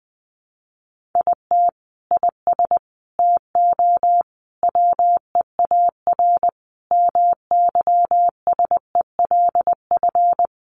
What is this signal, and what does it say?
Morse code, heard as a keyed tone, about 20 wpm, 700 Hz. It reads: IT IS TO WEAR MYSELF
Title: Sentences from Top 1500 Words 20wpm